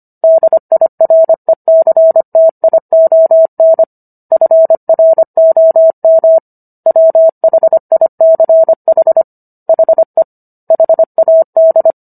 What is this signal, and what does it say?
Morse code, heard as a keyed tone, about 25 wpm, 650 Hz. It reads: DIRECTION FROM WHICH HE HAD